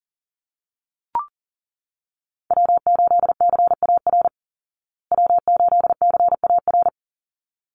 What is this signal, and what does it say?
Morse code, heard as a keyed tone, about 39 wpm, 700 Hz. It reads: W8CAR W8CAR
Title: Call Signs Repeated 3x - 40wpm